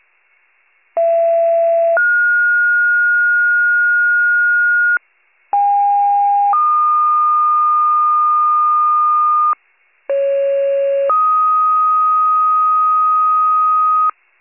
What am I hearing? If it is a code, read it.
Two-tone page, heard as a one-second tone, then a three-second tone, about 670 then 1430 Hz, about 810 then 1220 Hz, about 550 then 1250 Hz. It is heard from 1.0 to 14.1 s.